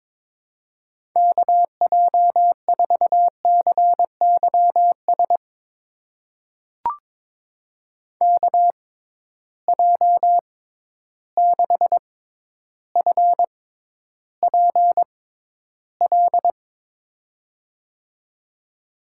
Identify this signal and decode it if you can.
Morse code, heard as a keyed tone, about 22 wpm, spaced as for 11 wpm, 700 Hz. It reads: KJ4CYS KJ6FPL
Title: USA Call Signs - 6 Characters - 6x Character Spacing - 22wpm